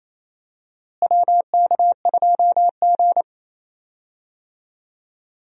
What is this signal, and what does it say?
Morse code, heard as a keyed tone, about 28 wpm, 700 Hz. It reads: WK2G
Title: USA Call Signs - 4 Characters - Repeated 3x - 28wpm